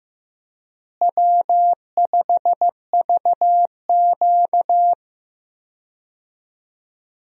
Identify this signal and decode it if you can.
Morse code, heard as a keyed tone, about 15 wpm, 700 Hz. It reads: W5VQ
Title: USA Call Signs - 4 Characters - Repeated 3x - 15wpm